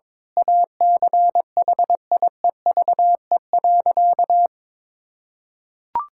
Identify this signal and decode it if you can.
Morse code, heard as a keyed tone, about 21 wpm, 700 Hz. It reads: ACHIEVE.
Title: Sentences from Top 1000 Words - Review Part 6 - 22wpm